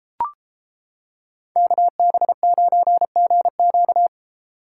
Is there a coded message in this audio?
Yes: Morse code from 1.6 to 4.1 s, heard as a keyed tone, about 29 wpm, 700 Hz. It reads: KB9GQ